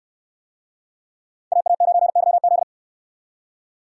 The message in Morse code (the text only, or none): AA9YZ